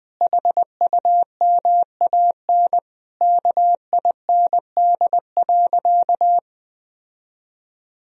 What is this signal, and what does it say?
Morse code, heard as a keyed tone, about 21 wpm, 700 Hz. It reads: HUMAN KIND.
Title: Sentences from Top 1000 Words - Review Part 4 - 20wpm